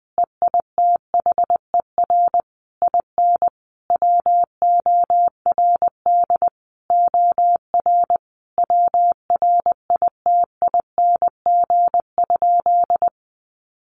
Morse code, heard as a keyed tone, about 20 wpm, 700 Hz. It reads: EITHER IN WORD OR WRITING?
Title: Sentences from Top 1200 Words 20wpm